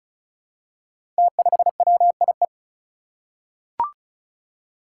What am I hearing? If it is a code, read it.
Morse code, heard as a keyed tone, about 36 wpm, 700 Hz. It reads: T5WIE
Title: Five Challenging Letter-Number Combinations 35wpm